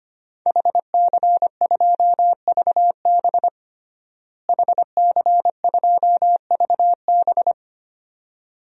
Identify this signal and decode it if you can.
Morse code, heard as a keyed tone, about 25 wpm, 700 Hz. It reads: HC2VB HC2VB
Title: DXCC-100 Call Signs - 5 Characters - Repeated 2x - 25wpm